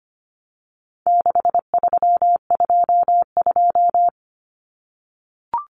Morse code, heard as a keyed tone, about 25 wpm, 700 Hz. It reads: 6322
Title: Numbers - 4-Digit 25wpm